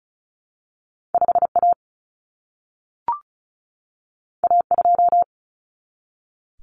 Morse code, heard as a keyed tone, about 36 wpm, 700 Hz. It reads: 5A A2
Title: Common Cut Numbers - 2-Digit 35wpm